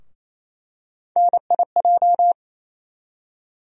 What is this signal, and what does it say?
Morse code, heard as a keyed tone, about 28 wpm, 700 Hz. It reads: NIJ
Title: USA Call Sign Postfixes - 3 Letters 28wpm